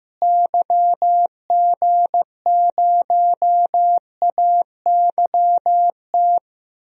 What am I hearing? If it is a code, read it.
Morse code, heard as a keyed tone, about 15 wpm, 700 Hz. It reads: YG0AYT
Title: DXCC-100 Call Signs - 6 Characters - Repeated 3x - 15wpm